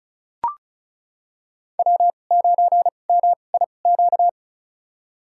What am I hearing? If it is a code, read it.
Morse code, heard as a keyed tone, about 36 wpm, spaced as for 25 wpm, 700 Hz. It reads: W9MIQ
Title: USA Call Signs - 2x Character Spacing - 35wpm